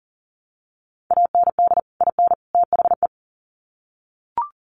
Morse code, heard as a keyed tone, about 40 wpm, 700 Hz. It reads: AND IN THE
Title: Sets of 3 Words 40wpm